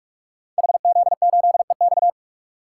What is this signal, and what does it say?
Morse code, heard as a keyed tone, about 44 wpm, 700 Hz. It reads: HZ8EX
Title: Code Groups - 5 Letter:Numbers - 45wpm